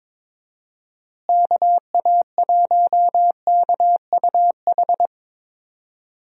Morse code, heard as a keyed tone, about 22 wpm, 700 Hz. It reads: KA1KUH
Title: USA Call Signs - 6 Characters - Repeated 3x - 22wpm